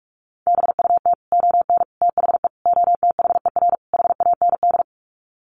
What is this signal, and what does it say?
Morse code, heard as a keyed tone, about 44 wpm, 700 Hz. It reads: BUT ON THE OTHER HAND
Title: Sets of 5 Words 45wpm